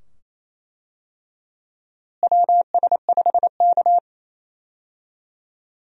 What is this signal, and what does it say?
Morse code, heard as a keyed tone, about 27 wpm, 700 Hz. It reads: WS5K
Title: USA Call Signs - 4 Characters - Repeated 2x - 28wpm